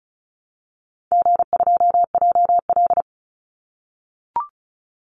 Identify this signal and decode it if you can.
Morse code, heard as a keyed tone, about 34 wpm, 700 Hz. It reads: G2JL
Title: Call Signs - 35wpm